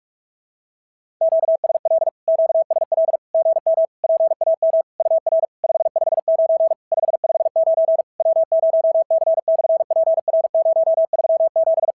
Morse code, heard as a keyed tone, about 44 wpm, 650 Hz. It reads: QSL QSL GM PAM UR 559 559 W0C/PR037
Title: SOTA QSO Practice 45wpm